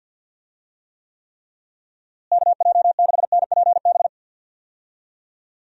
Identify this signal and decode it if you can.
Morse code, heard as a keyed tone, about 46 wpm, 700 Hz. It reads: KJ6NPB
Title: USA Call Signs - 6 Characters - Repeated 3x - 50wpm